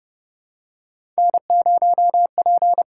Morse code, heard as a keyed tone, about 30 wpm, 700 Hz. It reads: N0P
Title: USA Call Signs - 5 Characters - 8x Character Spacing - 30wpm